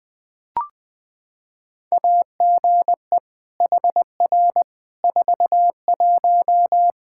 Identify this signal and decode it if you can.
Morse code, heard as a keyed tone, about 20 wpm, 700 Hz. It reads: AGE HR 41
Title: Practice QSO - Exchange 3 - More Info - 20wpm